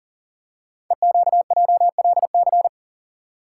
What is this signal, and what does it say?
Morse code, heard as a keyed tone, about 39 wpm, 700 Hz. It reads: EQJLC
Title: Code Groups - 5 Letter:Numbers - 40wpm